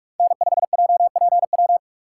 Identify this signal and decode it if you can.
Morse code, heard as a keyed tone, about 45 wpm, 700 Hz. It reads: N5JPW
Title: USA Call Signs - 5 Characters - 4x Character Spacing - 45wpm